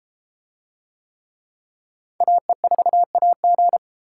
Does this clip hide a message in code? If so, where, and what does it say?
Yes: Morse code from 2.2 to 3.8 s, heard as a keyed tone, about 30 wpm, 700 Hz. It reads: AE4AG